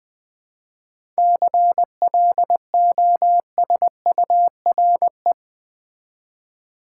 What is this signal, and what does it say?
Morse code, heard as a keyed tone, about 21 wpm, 690 Hz. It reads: CLOSURE
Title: 7 Letter Words - First 3 Letters Given - 20wpm